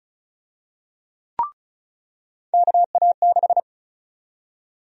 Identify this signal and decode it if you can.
Morse code, heard as a keyed tone, about 34 wpm, 700 Hz. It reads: KA6
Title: USA Call Sign Prefixes - 35wpm